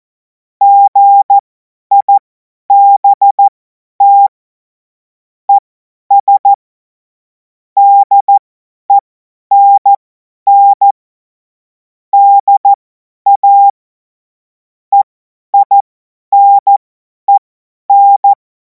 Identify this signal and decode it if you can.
Morse code, heard as a keyed tone, about 14 wpm, spaced as for 10 wpm, 800 Hz. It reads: GIBT ES DENN DA EINEN
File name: Morsekurs / Witze / Witz 19. Mai 2026 50 F70